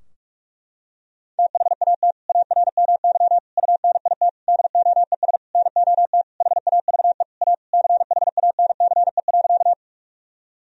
Morse code, heard as a keyed tone, about 47 wpm, 700 Hz. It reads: THAT ARMY UNIT DOES NOT HAVE A CHANCE.
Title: Sentences from Top 1000 Words - Review Part 2 - 45wpm